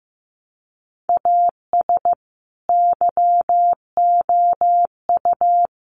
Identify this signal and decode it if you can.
Morse code, heard as a keyed tone, about 14 wpm, 700 Hz. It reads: AS YOU
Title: Sentences from Top 1100 Words 15wpm